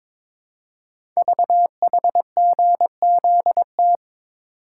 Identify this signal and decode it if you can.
Morse code, heard as a keyed tone, about 22 wpm, 700 Hz. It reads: VHGZT
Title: Code Groups - 5 Letters - 22wpm